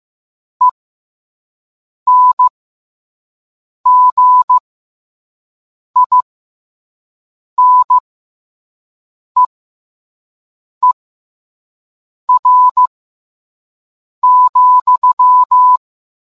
Morse code, heard as a keyed tone, about 15 wpm, spaced as for 5 wpm, 1000 Hz. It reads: ENGINEER,